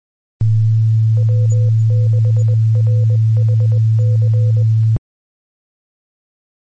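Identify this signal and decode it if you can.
Morse code, heard as a keyed tone, about 21 wpm, 500 Hz, with noise. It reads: W6RHC
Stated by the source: Morse code recorded on 146.850 kHz.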